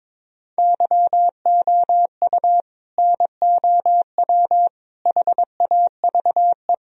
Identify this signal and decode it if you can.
Morse code, heard as a keyed tone, about 22 wpm, 700 Hz. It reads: YOU NOW HAVE
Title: Sentences from Top 1000 Words - Review Part 1 - 22wpm